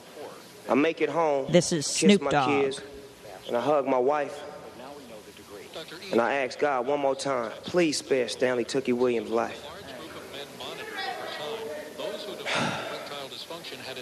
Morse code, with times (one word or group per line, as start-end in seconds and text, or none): none